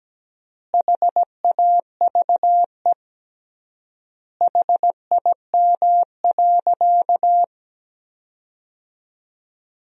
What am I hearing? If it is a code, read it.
Morse code, heard as a keyed tone, about 17 wpm, 700 Hz. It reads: HAVE HIM.
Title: 3x Word-Spacing - Sentences from Top 100 Words - 17wpm